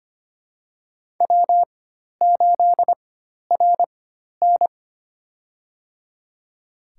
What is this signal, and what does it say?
Morse code, heard as a keyed tone, about 25 wpm, spaced as for 12 wpm, 700 Hz. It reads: W8RN
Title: USA Call Signs - 4 Characters - 4x Character Spacing - 25wpm